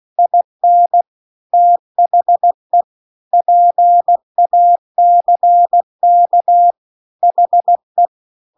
Morse code, heard as a keyed tone, about 16 wpm, 700 Hz. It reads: IN THE PACK HE
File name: CW-The-Big-Twoheaded-River-2-16WPM